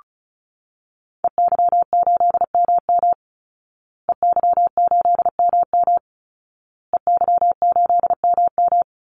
Morse code, heard as a keyed tone, about 34 wpm, 700 Hz. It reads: EY8MM EY8MM EY8MM
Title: Call Signs Repeated 2x - 35wpm